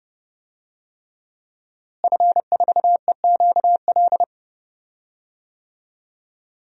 Morse code, heard as a keyed tone, about 32 wpm, 700 Hz. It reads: F4EQL
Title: DXCC-100 Call Signs - 5 Characters - Repeated 3x - 30wpm